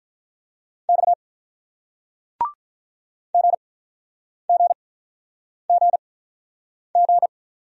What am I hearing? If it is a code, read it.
Morse code, heard as a keyed tone, about 49 wpm, spaced as for 9 wpm, 700 Hz. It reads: X GGGG